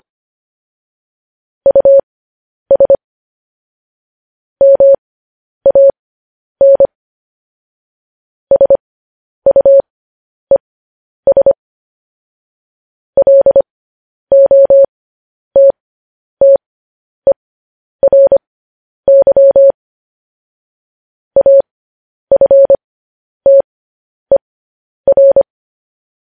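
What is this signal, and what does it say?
Morse code, heard as a keyed tone, about 25 wpm, spaced as for 10 wpm, 550 Hz. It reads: US MAN SUES LOTTERY AFTER
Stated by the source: Morse code news headlines.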